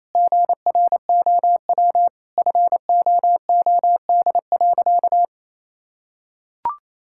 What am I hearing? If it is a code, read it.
Morse code, heard as a keyed tone, about 28 wpm, 700 Hz. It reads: GROW FOOD.
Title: Sentences from Top 1000 Words - Review Part 6 - 28wpm